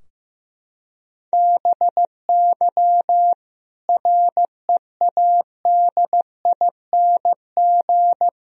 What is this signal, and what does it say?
Morse code, heard as a keyed tone, about 15 wpm, 700 Hz. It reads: BY READING
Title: Sentences from Top 1000 Words - Review Part 3 - 15wpm